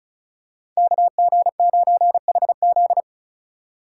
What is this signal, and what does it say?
Morse code, heard as a keyed tone, about 36 wpm, 700 Hz. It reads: KG9HZ
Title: USA Call Signs - 5 Characters - 3x Character Spacing - 35wpm